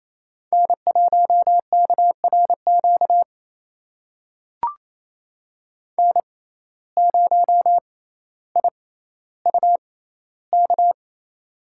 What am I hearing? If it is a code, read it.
Morse code, heard as a keyed tone, about 28 wpm, spaced as for 14 wpm, 700 Hz. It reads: N1KRQ N0IUK
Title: USA Call Signs - 5 Characters - 6x Character Spacing - 28wpm